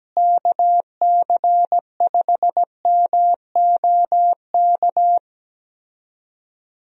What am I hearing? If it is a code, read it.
Morse code, heard as a keyed tone, about 17 wpm, 700 Hz. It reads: KC5MOK
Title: USA Call Signs - 6 Characters - Repeated 3x - 17wpm